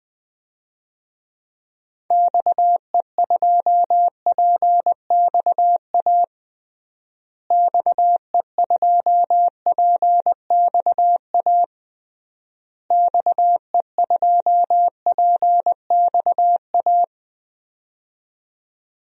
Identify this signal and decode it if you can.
Morse code, heard as a keyed tone, about 21 wpm, 700 Hz. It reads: XE2PXA XE2PXA XE2PXA
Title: DXCC-100 Call Signs - 6 Characters - Repeated 3x - 20wpm